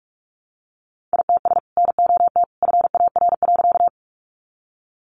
Morse code, heard as a keyed tone, about 45 wpm, 700 Hz. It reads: ITS NOT FAR.
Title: Sentences from Top 600 Words 45wpm